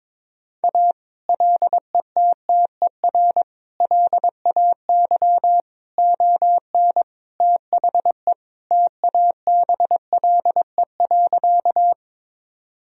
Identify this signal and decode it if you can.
Morse code, heard as a keyed tone, about 22 wpm, 700 Hz. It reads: A LETTER LAY ON THE TABLE.
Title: Sentences from Top 1000 Words - Review Part 2 - 22wpm